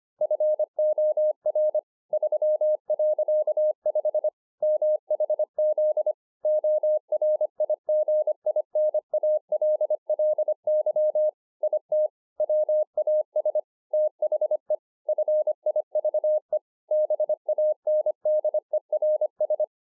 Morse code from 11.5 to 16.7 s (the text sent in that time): IT WAS THE FIVE